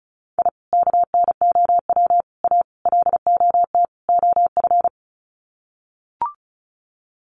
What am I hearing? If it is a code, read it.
Morse code, heard as a keyed tone, about 34 wpm, 700 Hz. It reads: I KNOW A LOT OF
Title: Sets of 5 Words Encore 35wpm